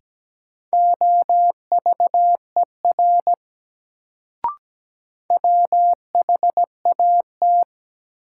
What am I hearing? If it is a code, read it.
Morse code, heard as a keyed tone, about 17 wpm, 700 Hz. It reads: OVER WHAT
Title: Sentences from Top 100 Words - New Word Drill - 17wpm